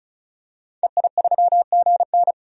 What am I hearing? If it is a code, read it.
Morse code, heard as a keyed tone, about 34 wpm, 700 Hz. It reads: EI3GN
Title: DXCC-100 Call Signs - 5 Characters - 35wpm